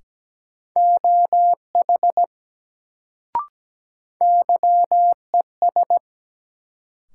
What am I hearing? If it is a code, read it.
Morse code, heard as a keyed tone, about 17 wpm, 700 Hz. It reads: OH YES
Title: Sentences from Top 500 Words - New Word Drill - 17wpm